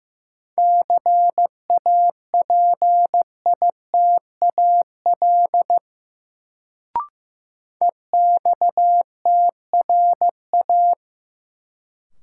Morse code, heard as a keyed tone, about 15 wpm, 700 Hz. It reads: CAPITAL EXTRA
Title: Sentences from Top 1000 Words - New Word Drill - 15wpm